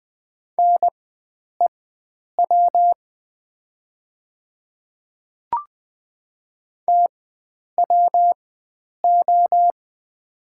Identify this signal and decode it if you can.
Morse code, heard as a keyed tone, about 20 wpm, spaced as for 9 wpm, 700 Hz. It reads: NEW TWO
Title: 4x Character Spacing - Top 300 Words - 20wpm